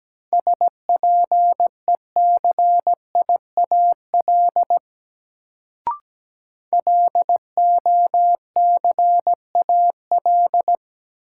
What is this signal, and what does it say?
Morse code, heard as a keyed tone, about 17 wpm, 700 Hz. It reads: SPECIAL LOCAL
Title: Sentences from Top 1000 Words - Review Part 3 - New Word Drill - 17wpm